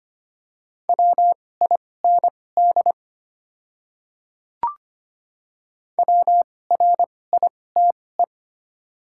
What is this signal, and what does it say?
Morse code, heard as a keyed tone, about 25 wpm, spaced as for 18 wpm, 700 Hz. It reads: WIND WRITE
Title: 2x Character Spacing - Top 500 Words - 25wpm